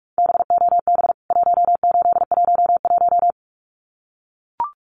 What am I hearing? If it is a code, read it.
Morse code, heard as a keyed tone, about 44 wpm, 700 Hz. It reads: BOB 1811
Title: CWT Exchange 45wpm